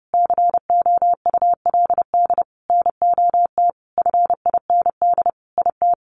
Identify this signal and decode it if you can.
Morse code, heard as a keyed tone, about 30 wpm, 700 Hz. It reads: COULD NOT FIND IT
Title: Sentences from Top 1400 Words 30wpm